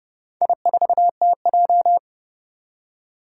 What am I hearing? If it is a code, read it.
Morse code, heard as a keyed tone, about 28 wpm, 700 Hz. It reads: I4TJ
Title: Four Challenging Letter-Number Combinations 30wpm